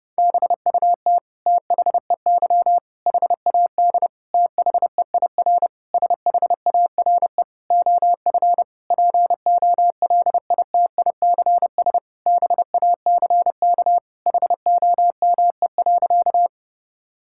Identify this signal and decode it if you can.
Morse code, heard as a keyed tone, about 30 wpm, 700 Hz. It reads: BUT THEY HAD THEIR SHARE OF POLITICS BACK HOME.
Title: Sentences from Top 1000 Words - Review Part 6 - 30wpm